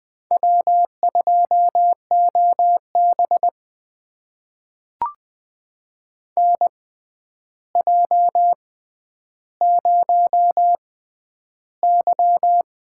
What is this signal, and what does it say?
Morse code, heard as a keyed tone, about 20 wpm, spaced as for 10 wpm, 700 Hz. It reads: W2OB NJ0Y
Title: USA Call Signs - 6x Character Spacing - 20wpm